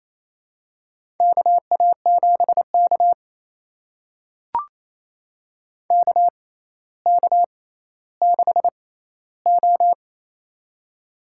Morse code, heard as a keyed tone, about 28 wpm, spaced as for 14 wpm, 700 Hz. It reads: KA7K KK6O
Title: USA Call Signs - 4 Characters - 6x Character Spacing - 28wpm